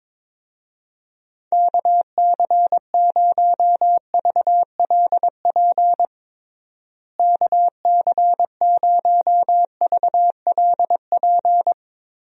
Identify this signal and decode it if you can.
Morse code, heard as a keyed tone, about 22 wpm, 700 Hz. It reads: KC0VLP KC0VLP
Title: USA Call Signs - 6 Characters - Repeated 2x - 22wpm